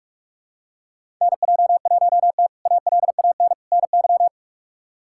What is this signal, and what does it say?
Morse code, heard as a keyed tone, about 46 wpm, 700 Hz. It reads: NJ1T ALAN NY
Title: K1USN SST QSO Practice - 45wpm